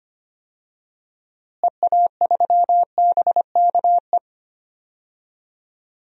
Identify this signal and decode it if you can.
Morse code, heard as a keyed tone, about 25 wpm, 700 Hz. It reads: EA3BKE
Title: DXCC-100 Call Signs - 6 Characters - Repeated 3x - 25wpm